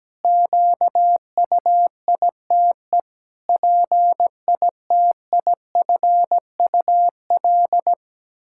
Morse code, heard as a keyed tone, about 17 wpm, 700 Hz. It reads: QUITE PITIFUL
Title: Pangram Challenge v2 - 17wpm